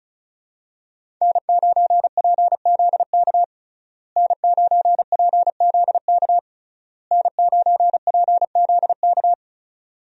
Morse code, heard as a keyed tone, about 36 wpm, 700 Hz. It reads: N9PZK N9PZK N9PZK
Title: USA Call Signs - 5 Characters - Repeated 3x - 35wpm